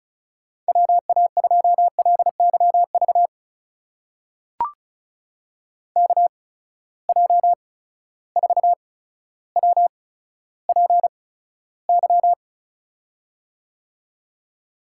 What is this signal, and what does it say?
Morse code, heard as a keyed tone, about 36 wpm, spaced as for 15 wpm, 700 Hz. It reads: WA2LYV KJ4WPY
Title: USA Call Signs - 8x Character Spacing - 35wpm